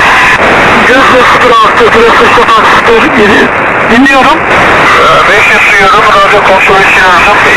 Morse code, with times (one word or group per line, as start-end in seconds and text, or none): none